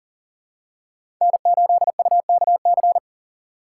Morse code, heard as a keyed tone, about 40 wpm, 700 Hz. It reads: N8UKC
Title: USA Call Signs - 5 Characters - 4x Character Spacing - 40wpm